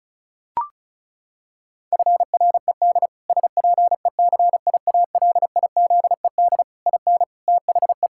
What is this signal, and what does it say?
Morse code, heard as a keyed tone, about 34 wpm, 700 Hz. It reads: FRED SPECIALIZED IN THE
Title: Pangram Challenge v2 - 35wpm